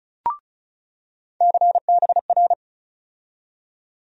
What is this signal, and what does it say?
Morse code, heard as a keyed tone, about 35 wpm, 700 Hz. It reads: CBR
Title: USA Call Sign Postfixes - 3 Letters 35wpm